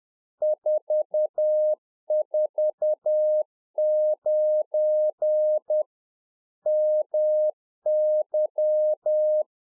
Morse code, heard as a keyed tone, about 10 wpm, 610 Hz. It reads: 449 MY
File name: UNFOGETTABLE QSO 10WPM